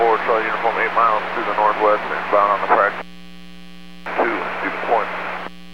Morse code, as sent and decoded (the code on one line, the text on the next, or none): none